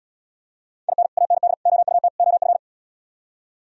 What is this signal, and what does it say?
Morse code, heard as a keyed tone, about 77 wpm, 700 Hz. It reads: IN AND OUT OF